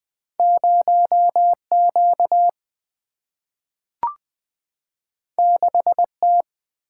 Morse code, heard as a keyed tone, about 20 wpm, 700 Hz. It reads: 0Q 6T
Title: Single-Letter Single-Number 20wpm